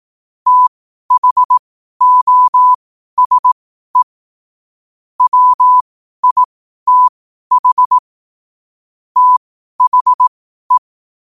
Morse code, heard as a keyed tone, about 18 wpm, spaced as for 12 wpm, 1000 Hz. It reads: THOSE WITH THE
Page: Quotes for Wed, 13 Aug 2025 in Morse Code at 12 words per minute.